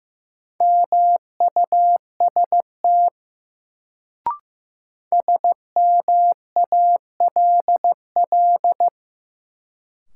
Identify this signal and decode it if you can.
Morse code, heard as a keyed tone, about 15 wpm, 700 Hz. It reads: MUST SMALL
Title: Sets of 2 Words - New Word Drill - 15wpm